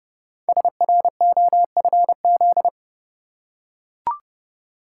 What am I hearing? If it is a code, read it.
Morse code, heard as a keyed tone, about 30 wpm, 700 Hz. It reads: SROFZ
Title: Code Groups - 5 Letters - 30wpm